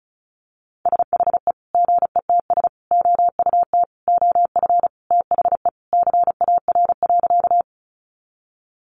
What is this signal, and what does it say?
Morse code, heard as a keyed tone, about 34 wpm, 700 Hz. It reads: SHE GETS OUT OF THE CAR.
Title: Easy Phrases 35wpm